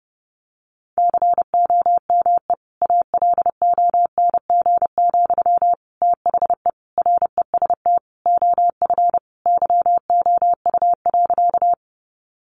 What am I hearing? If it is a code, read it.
Morse code, heard as a keyed tone, about 29 wpm, 700 Hz. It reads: COME ALONG, THE REST OF YOU.
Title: Sentences from Top 700 Words 30wpm